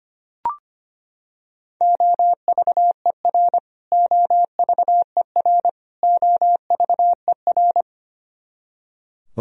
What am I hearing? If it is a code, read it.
Morse code, heard as a keyed tone, about 25 wpm, 700 Hz. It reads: OVER OVER OVER
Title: Top 100 Words Repeated 25wpm